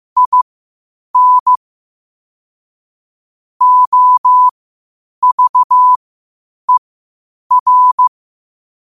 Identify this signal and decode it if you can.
Morse code, heard as a keyed tone, about 15 wpm, spaced as for 8 wpm, 1000 Hz. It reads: IN OVER
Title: 8 WPM morse code quotes for Fri, 01 Mar 2024